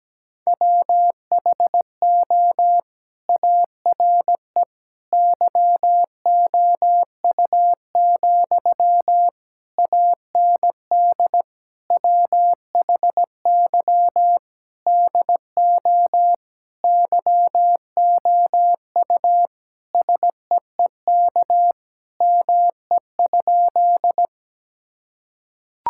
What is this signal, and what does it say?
Morse code, heard as a keyed tone, about 17 wpm, 700 Hz. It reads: WHO ARE YOU, AND WHY DO YOU SEEK ME?
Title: Sentences from Top 1600 Words 17wpm